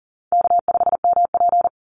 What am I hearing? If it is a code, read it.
Morse code, heard as a keyed tone, about 40 wpm, 700 Hz. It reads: K5MP
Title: Easy Call Signs Repeated 3x - 40wpm